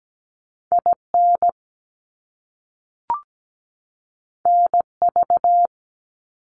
Morse code, heard as a keyed tone, about 17 wpm, 700 Hz. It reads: IN NV
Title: US States 17wpm